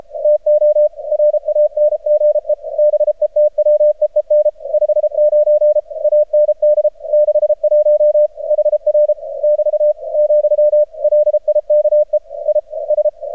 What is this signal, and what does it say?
Morse code, heard as a keyed tone, about 33 wpm, 600 Hz, with radio noise. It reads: TO RANGE BETWEEN 59 AND 61 HR <BT> , LIKE I S
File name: archived / CRSnet